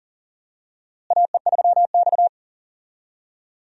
Morse code, heard as a keyed tone, about 40 wpm, 700 Hz. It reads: AE3X
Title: USA Call Signs - 4 Characters - Repeated 2x - 40wpm